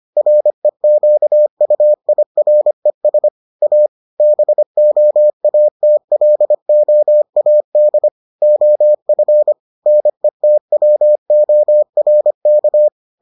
Morse code, heard as a keyed tone, about 25 wpm, 600 Hz. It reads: REQUIRES A BOATLOAD OF NETWORK